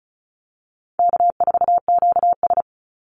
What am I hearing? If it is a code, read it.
Morse code, heard as a keyed tone, about 34 wpm, 700 Hz. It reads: K4QS
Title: NA SOTA Chaser Callsigns 35wpm